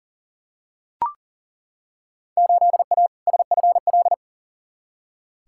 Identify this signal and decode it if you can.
Morse code, heard as a keyed tone, about 40 wpm, 700 Hz. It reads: 8A SFL
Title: Field Day Exchanges Encore - 40wpm